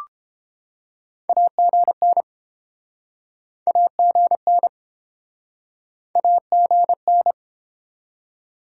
Morse code, heard as a keyed tone, about 30 wpm, 700 Hz. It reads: AGN AGN AGN